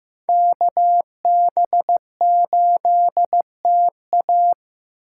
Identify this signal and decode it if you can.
Morse code, heard as a keyed tone, about 18 wpm, 700 Hz. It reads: KB8TA
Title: Speed Racing - USA 2x2 Call Signs 15wpm